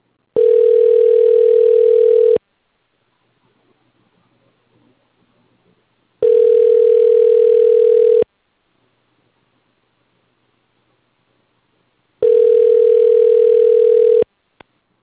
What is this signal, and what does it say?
Ringback tone (North America), 440 + 480 Hz.